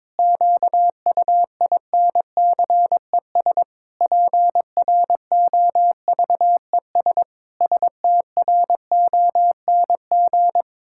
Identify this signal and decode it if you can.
Morse code, heard as a keyed tone, about 22 wpm, 700 Hz. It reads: QUINCES PROVES STRONG
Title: Pangram Challenge v2 - 22wpm